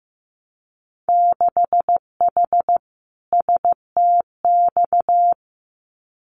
Morse code, heard as a keyed tone, about 15 wpm, 700 Hz. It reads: 6H STX
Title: Winter Field Day Exchanges Encore 15wpm